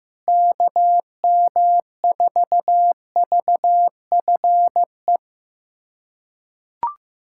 Morse code, heard as a keyed tone, about 15 wpm, 700 Hz. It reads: KM4VFE
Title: USA Callsign DE Callsign - 6 Characters - 15wpm